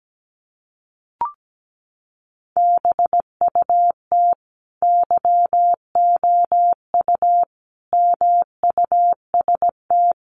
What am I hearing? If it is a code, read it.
Morse code, heard as a keyed tone, about 17 wpm, 700 Hz. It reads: BUT YOU MUST
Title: Sentences from Top 1300 Words 17wpm